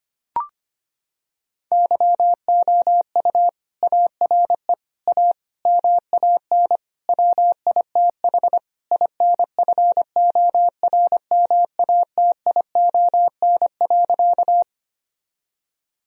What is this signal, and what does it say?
Morse code, heard as a keyed tone, about 25 wpm, 700 Hz. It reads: YOU ARE A MAN WITH INFORMATION.
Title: Sentences from Top 1000 Words - Review Part 3 - 25wpm